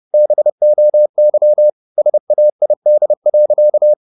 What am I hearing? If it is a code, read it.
Morse code, heard as a keyed tone, about 30 wpm, 600 Hz. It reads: BOY SAID.